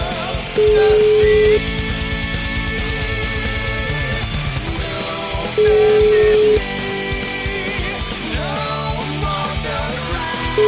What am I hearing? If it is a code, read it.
Ringback tone (Europe, 425 Hz), with noise.